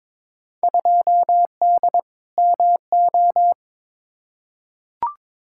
Morse code, heard as a keyed tone, about 22 wpm, 700 Hz. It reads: 2D MO
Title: Field Day Exchanges - 22wpm